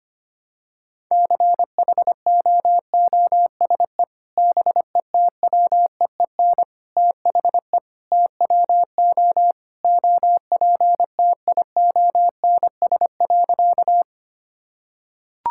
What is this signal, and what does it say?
Morse code, heard as a keyed tone, about 25 wpm, 700 Hz. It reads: CHOOSE BETWEEN THE TWO OPTIONS.
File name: practice / 25wpm / Sentences with 3-Word Phrases - 25wpm